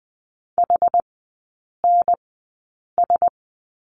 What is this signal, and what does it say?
Morse code, heard as a keyed tone, about 20 wpm, spaced as for 8 wpm, 700 Hz. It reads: HNS
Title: 159 Lesson 26e - Letter g - Three Characters 20wpm